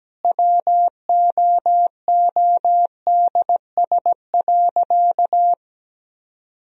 Morse code, heard as a keyed tone, about 17 wpm, 700 Hz. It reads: WOODS.
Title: Sentences from Top 1000 Words - Review Part 2 - 17wpm